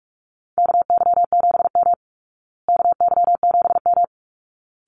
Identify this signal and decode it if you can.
Morse code, heard as a keyed tone, about 44 wpm, 700 Hz. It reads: KY7M KY7M
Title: Call Signs Repeated 3x - 45wpm